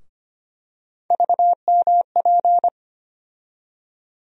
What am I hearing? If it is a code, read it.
Morse code, heard as a keyed tone, about 25 wpm, 700 Hz. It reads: VMP
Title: USA Call Sign Postfixes - 3 Letters 25wpm